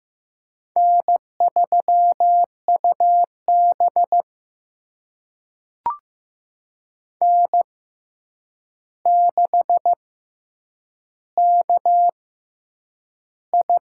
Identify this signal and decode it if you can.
Morse code, heard as a keyed tone, about 15 wpm, spaced as for 8 wpm, 700 Hz. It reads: N3UB N6KI
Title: USA Call Signs - 4 Characters - 6x Character Spacing - 15wpm